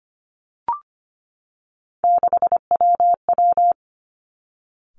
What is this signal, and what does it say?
Morse code, heard as a keyed tone, about 25 wpm, 700 Hz. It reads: 6WW
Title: Three Challenging Letter-Number Combinations 25wpm